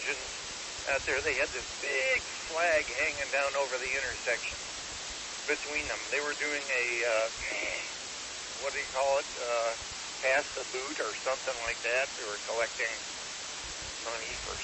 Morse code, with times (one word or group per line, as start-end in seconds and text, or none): none